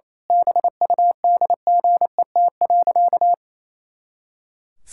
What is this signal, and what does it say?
Morse code, heard as a keyed tone, about 27 wpm, 700 Hz. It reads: BUDGET.
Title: Pangram Challenge v2 - 28wpm